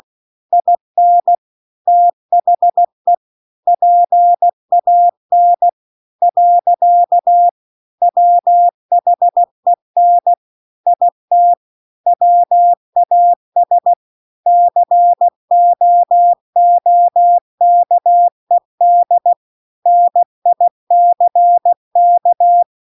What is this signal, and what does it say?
Morse code, heard as a keyed tone, about 16 wpm, 700 Hz. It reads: IN THE PAN . WHEN IT WAS COOKED NICK
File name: CW-The-Big-Twoheaded-River-2-16WPM